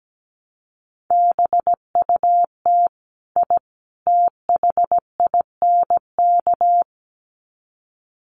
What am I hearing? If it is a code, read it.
Morse code, heard as a keyed tone, about 17 wpm, 700 Hz. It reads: BUT I THINK
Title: Sets of 3 Words 17wpm